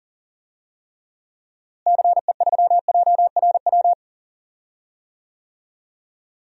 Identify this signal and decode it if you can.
Morse code, heard as a keyed tone, about 40 wpm, 700 Hz. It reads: CE3JRW
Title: DXCC-100 Call Signs - 6 Characters - Repeated 2x - 40wpm